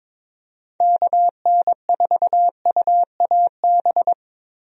Morse code, heard as a keyed tone, about 22 wpm, 700 Hz. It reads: KN4UAB
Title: USA Call Signs - 6 Characters - Repeated 2x - 22wpm